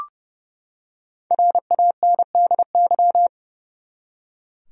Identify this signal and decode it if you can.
Morse code, heard as a keyed tone, about 30 wpm, 700 Hz. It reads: RANDY
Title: USA Common Names - Max 5 Letters - 30wpm